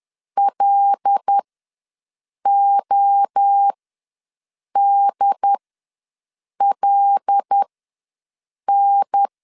Morse code, heard as a keyed tone, about 10 wpm, spaced as for 6 wpm, 800 Hz. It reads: LODLN
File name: LEZIONI / 5 LBVC / 5 L ws 10wpm 3636